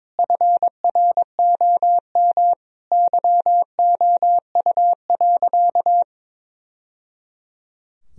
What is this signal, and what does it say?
Morse code, heard as a keyed tone, about 22 wpm, 700 Hz. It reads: FROM YOU.
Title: Sentences from Top 1000 Words - Review Part 2 - 22wpm